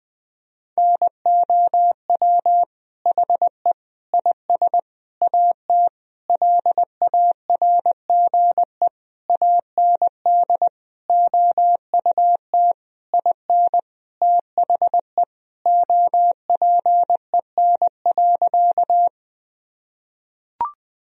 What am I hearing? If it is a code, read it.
Morse code, heard as a keyed tone, about 20 wpm, 700 Hz. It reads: NOW HE IS AT LARGE AND OUT IN THE OPEN.
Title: Sentences from Top 1000 Words - Review Part 1 - 20wpm